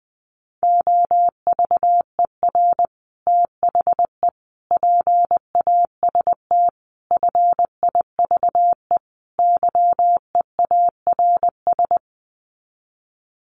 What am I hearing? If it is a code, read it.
Morse code, heard as a keyed tone, about 19 wpm, 700 Hz. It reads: OVER THE PAST FIVE YEARS
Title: Sets of 5 Words Encore 20wpm